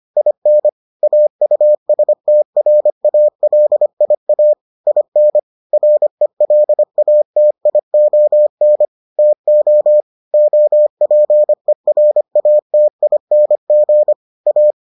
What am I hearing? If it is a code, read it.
Morse code, heard as a keyed tone, about 25 wpm, 600 Hz. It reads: IN AUSTRALIA IN RELATION TO OPERATING A